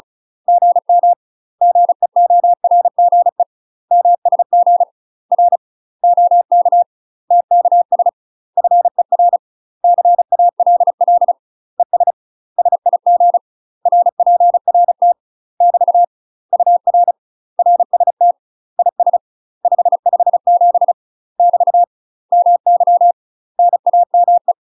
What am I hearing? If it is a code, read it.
Morse code, heard as a keyed tone, about 36 wpm, 700 Hz. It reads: GM GEORGE MSG R OK TKS FER CALL ES SIG RPRT <BT> UR RST IS 557 <BT> MY NAME
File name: UK MASTER QSOs TRAINING / 35WPM / UK MASTER QSOs 0024